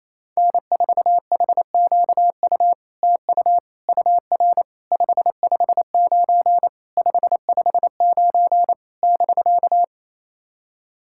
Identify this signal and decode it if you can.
Morse code, heard as a keyed tone, about 27 wpm, 700 Hz. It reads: N4HQU TU UR 559 559 <BK>
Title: SOTA QSO Practice 28wpm